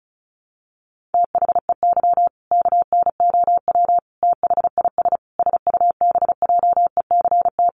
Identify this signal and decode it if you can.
Morse code, heard as a keyed tone, about 34 wpm, 700 Hz. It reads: THEY KNOW THIS SUBJECT
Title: Sentences from Top 800 Words 35wpm